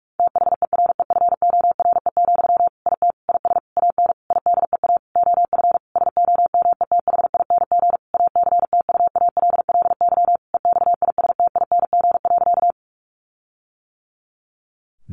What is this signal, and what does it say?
Morse code, heard as a keyed tone, about 44 wpm, 700 Hz. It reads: THEREFORE, IT IS AN IDEA OF SOMETHING ACTUALLY EXISTING.
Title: Sentences from Top 1500 Words 45wpm